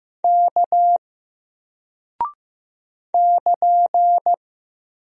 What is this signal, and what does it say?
Morse code, heard as a keyed tone, about 15 wpm, 700 Hz. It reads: K <KN>
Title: Single Character (Letter:Number:Punctuation:Prosign) Alt 15wpm